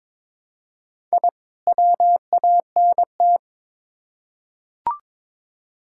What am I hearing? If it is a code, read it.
Morse code, heard as a keyed tone, about 22 wpm, 700 Hz. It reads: I WANT
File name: Build A Phrase - 2-Word Phrases - 22wpm